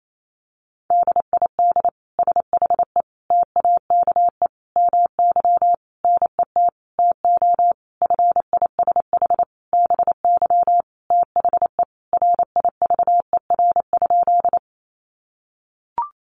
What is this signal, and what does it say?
Morse code, heard as a keyed tone, about 28 wpm, 700 Hz. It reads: DID SHE TAKE MY NET TO FISH BY THE RIVER?
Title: Sentences from Top 1400 Words 28wpm